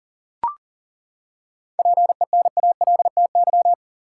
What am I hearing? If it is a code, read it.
Morse code, heard as a keyed tone, about 40 wpm, 700 Hz. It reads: PENALTY
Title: Seven Letter Words - 40wpm